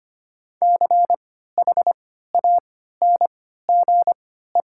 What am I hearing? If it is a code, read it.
Morse code, heard as a keyed tone, about 25 wpm, spaced as for 14 wpm, 700 Hz. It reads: CHANGE
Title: 3x Character Spacing - Top 200 Words - 25wpm